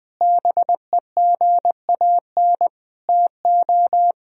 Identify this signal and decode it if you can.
Morse code, heard as a keyed tone, about 20 wpm, 700 Hz. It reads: BEGAN TO
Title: Sentences from Top 1000 Words - Review Part 1 - 20wpm